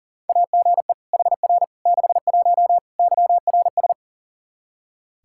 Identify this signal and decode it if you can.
Morse code, heard as a keyed tone, about 40 wpm, 700 Hz. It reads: AGE HR 61 YRS
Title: Practice QSO - Exchange 3 - More Info - 40wpm